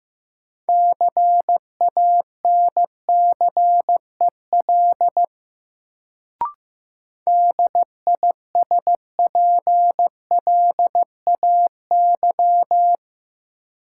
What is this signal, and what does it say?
Morse code, heard as a keyed tone, about 15 wpm, 700 Hz. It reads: CANCEL DISPLAY
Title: Sentences from Top 1200 Words - New Word Drill - 15wpm